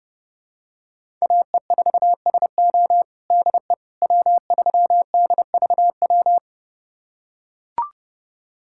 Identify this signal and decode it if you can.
Morse code, heard as a keyed tone, about 31 wpm, 700 Hz. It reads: AE4SO DE W3DVW
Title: USA Callsign DE Callsign - 5 Characters - 30wpm